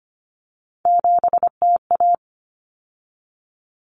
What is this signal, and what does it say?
Morse code, heard as a keyed tone, about 25 wpm, 700 Hz. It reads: 7TA
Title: Common Cut Numbers - 3-Digit 25wpm